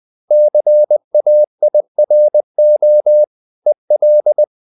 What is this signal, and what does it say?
Morse code, heard as a keyed tone, about 20 wpm, 600 Hz. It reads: CAIRO EL
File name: CW / Practice / NationalCapitals20